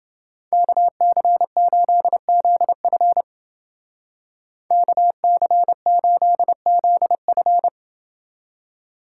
Morse code, heard as a keyed tone, about 29 wpm, 700 Hz. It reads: KC8ZF KC8ZF